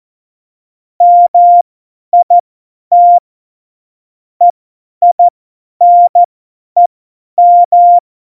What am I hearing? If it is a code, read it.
Morse code, heard as a keyed tone, about 14 wpm, spaced as for 10 wpm, 700 Hz. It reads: MIT EINEM
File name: Morsekurs / Witze / Witz 12. Februar 2019, 50 F70